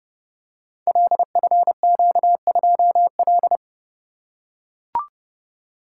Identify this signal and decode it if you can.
Morse code, heard as a keyed tone, about 30 wpm, 700 Hz. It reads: LFQ2L
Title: Code Groups - 5 Letter:Numbers - 30wpm